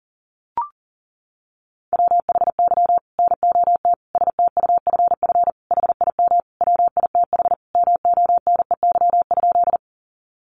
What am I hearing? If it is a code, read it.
Morse code, heard as a keyed tone, about 40 wpm, 700 Hz. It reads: WHY NOT STUFF HIM WITH MONEY?
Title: Sentences from Top 1400 Words 40wpm